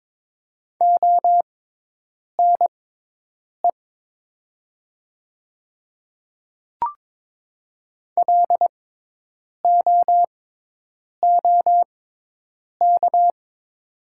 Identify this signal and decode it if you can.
Morse code, heard as a keyed tone, about 22 wpm, spaced as for 8 wpm, 700 Hz. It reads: ONE LOOK
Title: 6x Character Spacing - Top 100 Words - 22wpm